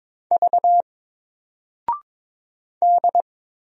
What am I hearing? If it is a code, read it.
Morse code, heard as a keyed tone, about 22 wpm, 700 Hz. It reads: V D
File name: Single-Letter-Number-22wpm